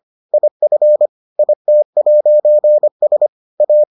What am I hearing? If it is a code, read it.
Morse code, heard as a keyed tone, about 25 wpm, 600 Hz. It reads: IF IT'S A